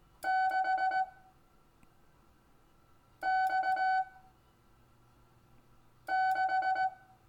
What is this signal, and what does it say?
Morse code, heard as a keyed tone, about 18 wpm, 760 Hz. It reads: 6 X 6
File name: cw / Lesson25 6 X